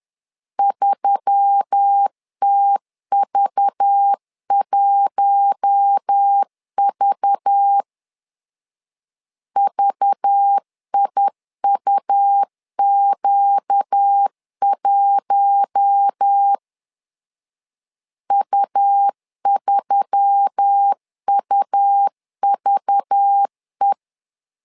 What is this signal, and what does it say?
Morse code, heard as a keyed tone, about 10 wpm, 800 Hz. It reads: 3TV1V VIUQ1 U3UVE
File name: LEZIONI / 5 LBVC / 5 LBV 10wpm 2248